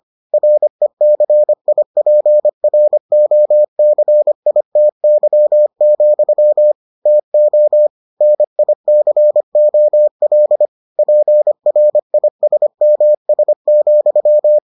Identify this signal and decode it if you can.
Morse code, heard as a keyed tone, about 25 wpm, 600 Hz. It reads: RECIPROCITY, TO NICOL PRISMS,